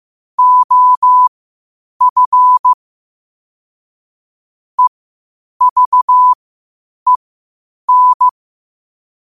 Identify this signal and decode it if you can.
Morse code, heard as a keyed tone, about 15 wpm, spaced as for 8 wpm, 1000 Hz. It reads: OF EVEN
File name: qotdfiles / 8wpm